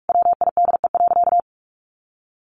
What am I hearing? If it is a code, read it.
Morse code, heard as a keyed tone, about 44 wpm, 700 Hz. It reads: WIDE.
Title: Sentences from Top 1200 Words 45wpm